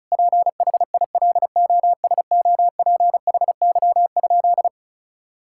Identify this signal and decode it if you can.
Morse code, heard as a keyed tone, about 34 wpm, 700 Hz. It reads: PHILOSOPHY?
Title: Sentences from Top 1000 Words - Review Part 6 - 35wpm